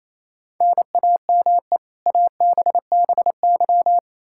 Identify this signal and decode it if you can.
Morse code, heard as a keyed tone, about 28 wpm, 700 Hz. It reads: NAME ABBY
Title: Practice QSO - Exchange 1 - Beginning - 28wpm